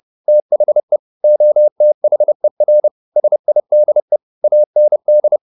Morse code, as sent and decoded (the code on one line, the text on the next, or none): - .... / --- - .... .-. / ... .. -.. / .- -. -..
THE OTHER SIDE AND